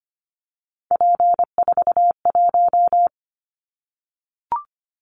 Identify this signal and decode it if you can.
Morse code, heard as a keyed tone, about 25 wpm, 700 Hz. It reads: P41
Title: Callsign Prefixes for Top 100 DXCC - 25wpm